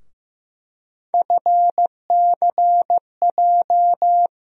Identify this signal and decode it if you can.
Morse code, heard as a keyed tone, about 15 wpm, 700 Hz. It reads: FCJ
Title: USA Call Sign Postfixes - 3 Letters 15wpm